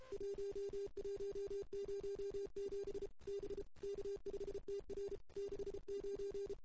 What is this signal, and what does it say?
Morse code, heard as a keyed tone, about 29 wpm, 440 Hz. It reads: 1107 B K5TR 69
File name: CW / 2008-ss-cw-k5tr-op-n5rz -14:00